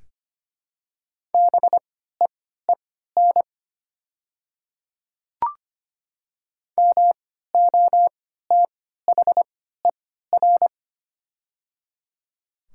Morse code, heard as a keyed tone, about 25 wpm, spaced as for 14 wpm, 700 Hz. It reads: BEEN MOTHER
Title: 3x Character Spacing - Top 200 Words - 25wpm